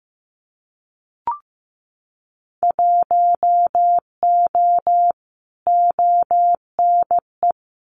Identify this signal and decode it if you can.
Morse code, heard as a keyed tone, about 15 wpm, 700 Hz. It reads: 1O ONE
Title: Winter Field Day Exchanges 15wpm